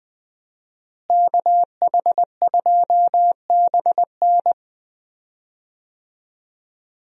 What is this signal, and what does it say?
Morse code, heard as a keyed tone, about 20 wpm, 700 Hz. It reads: KH2BN
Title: USA Call Signs - 5 Characters - Repeated 2x - 20wpm